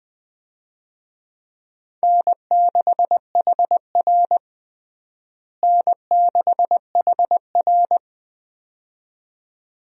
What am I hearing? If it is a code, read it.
Morse code, heard as a keyed tone, about 20 wpm, 700 Hz. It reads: N6HR N6HR
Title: USA Call Signs - 4 Characters - Repeated 2x - 20wpm